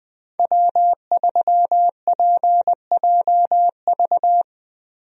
Morse code, heard as a keyed tone, about 21 wpm, 700 Hz. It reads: W3PJV
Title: USA Call Signs - 5 Characters - Repeated 3x - 20wpm